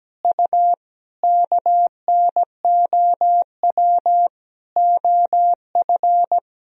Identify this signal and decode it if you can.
Morse code, heard as a keyed tone, about 17 wpm, 700 Hz. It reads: U KNOW OF
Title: Sentences from Top 100 Words and Abbreviations 17wpm